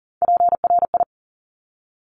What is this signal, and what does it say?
Morse code, heard as a keyed tone, about 41 wpm, 700 Hz. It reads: PRI
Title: Top Three Letter Combinations in Words 40wpm